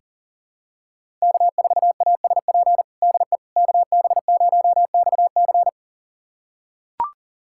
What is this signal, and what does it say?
Morse code, heard as a keyed tone, about 40 wpm, 700 Hz. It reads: K4ASP DE KB0XC
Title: USA Callsign DE Callsign - 5 Characters - 40wpm